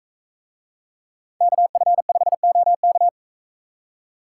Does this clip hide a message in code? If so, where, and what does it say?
Yes: Morse code from 1.4 to 3.1 s, heard as a keyed tone, about 44 wpm, 700 Hz. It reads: KF5OK